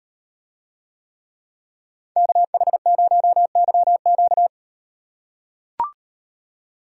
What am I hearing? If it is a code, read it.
Morse code, heard as a keyed tone, about 38 wpm, 700 Hz. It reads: KH0YQ